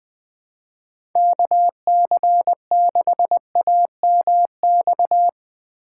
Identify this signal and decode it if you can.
Morse code, heard as a keyed tone, about 20 wpm, 700 Hz. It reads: KC6AMX
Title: USA Call Signs - 6 Characters - 2x Character Spacing - 20wpm